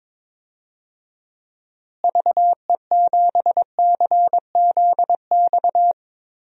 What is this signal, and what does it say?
Morse code, heard as a keyed tone, about 22 wpm, 700 Hz. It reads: VE7CZX
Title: DXCC-100 Call Signs - 6 Characters - Repeated 2x - 22wpm